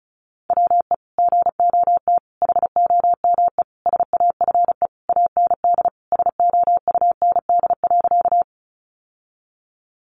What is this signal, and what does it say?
Morse code, heard as a keyed tone, about 34 wpm, 700 Hz. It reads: WE GOT HOME SAFE AND SOUND.
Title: Sentences from Top 900 Words 35wpm